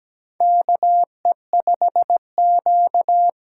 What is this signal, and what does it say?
Morse code, heard as a keyed tone, about 17 wpm, 700 Hz. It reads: KE5Q
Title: USA Call Signs - 6 Characters - 8x Character Spacing - 17wpm